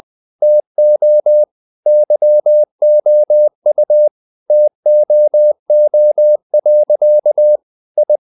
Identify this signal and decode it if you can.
Morse code, heard as a keyed tone, about 20 wpm, 600 Hz. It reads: TO YOU TOO. I
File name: the old man and the sea part 2 20wpm